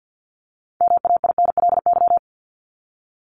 Morse code, heard as a keyed tone, about 50 wpm, 700 Hz. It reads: MAINLY
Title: Top 1251-1500 Words 50wpm